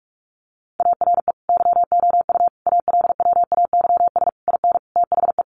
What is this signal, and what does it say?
Morse code, heard as a keyed tone, about 44 wpm, 700 Hz. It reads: ARE YOU ALWAYS IN THE
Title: Sentences from Top 1400 Words 45wpm